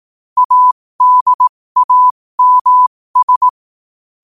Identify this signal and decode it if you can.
Morse code, heard as a keyed tone, about 18 wpm, spaced as for 15 wpm, 1000 Hz. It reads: ADAMS
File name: qotdfiles / 15wpm